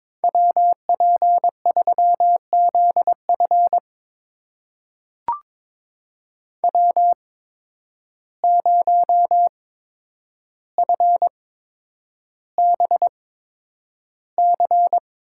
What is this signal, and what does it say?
Morse code, heard as a keyed tone, about 22 wpm, spaced as for 9 wpm, 700 Hz. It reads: WP3ZF W0FBC
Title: USA Call Signs - 5 Characters - 8x Character Spacing - 22wpm